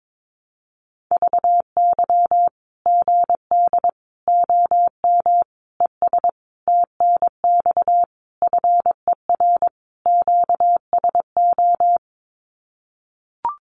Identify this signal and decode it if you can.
Morse code, heard as a keyed tone, about 22 wpm, 700 Hz. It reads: VY GD OM ES TNX FER QSO
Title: Common QSO Phrases 22wpm